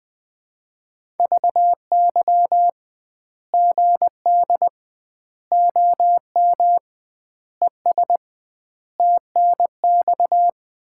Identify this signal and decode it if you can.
Morse code, heard as a keyed tone, about 20 wpm, 700 Hz. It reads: VY GD OM ES TNX
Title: Common QSO Phrases - 2x Word-Spacing - 20wpm